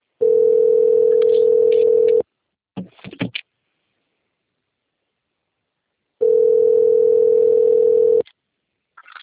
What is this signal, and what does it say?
Ringback tone (North America), 440 + 480 Hz.
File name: out- - - - - .29685